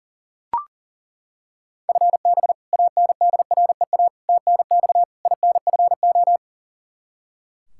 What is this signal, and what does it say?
Morse code, heard as a keyed tone, about 37 wpm, 700 Hz. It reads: FB ANDREA TNX INFO
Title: Practice QSO - Exchange 3 - More Info - 40wpm